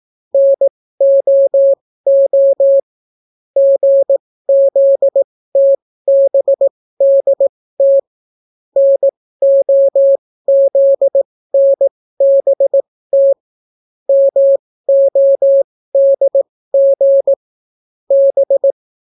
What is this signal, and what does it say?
Morse code, heard as a keyed tone, about 18 wpm, spaced as for 14 wpm, 550 Hz. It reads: NOO GZTBDT NOZNBT MODG B